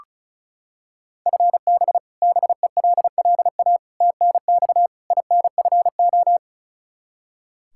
Morse code, heard as a keyed tone, about 36 wpm, 700 Hz. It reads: FB BELLA TNX INFO
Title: Practice QSO - Exchange 3 - More Info - 35wpm